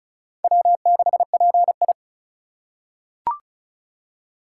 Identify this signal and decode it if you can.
Morse code, heard as a keyed tone, about 36 wpm, 700 Hz. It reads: W6PI
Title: USA Call Signs - 4 Characters - 3x Character Spacing - 35wpm